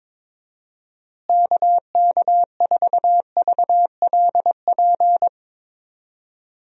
Morse code, heard as a keyed tone, about 22 wpm, 700 Hz. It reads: KK4VLP
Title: USA Call Signs - 6 Characters - Repeated 3x - 22wpm